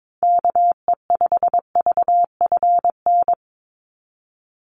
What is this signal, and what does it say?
Morse code, heard as a keyed tone, about 22 wpm, 700 Hz. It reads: KE5VFN
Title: USA Call Signs 22wpm